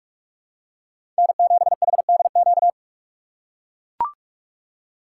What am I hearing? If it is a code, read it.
Morse code, heard as a keyed tone, about 43 wpm, 700 Hz. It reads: N7HDQ
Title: USA Call Signs - 3x Character Spacing - 45wpm